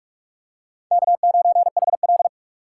Morse code, heard as a keyed tone, about 46 wpm, 700 Hz. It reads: K9HL
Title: USA Call Signs - 4 Characters - 2x Character Spacing - 45wpm